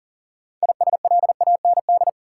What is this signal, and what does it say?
Morse code, heard as a keyed tone, about 40 wpm, 700 Hz. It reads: ISLAND
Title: Sentences from Top 500 Words - New Word Drill - 40wpm